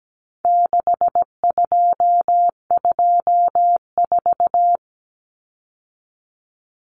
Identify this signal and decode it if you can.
Morse code, heard as a keyed tone, about 17 wpm, 700 Hz. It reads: 6224
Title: Common Cut Numbers - 4-Digit 17wpm